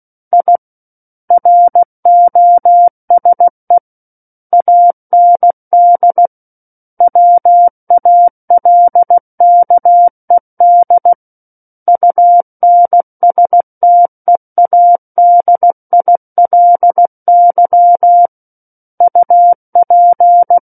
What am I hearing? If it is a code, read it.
Morse code, heard as a keyed tone, about 16 wpm, 700 Hz. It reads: I ROSE AND WALKED UNSTEADILY UP
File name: Book One / 07-Chapter 7 - 16 WPM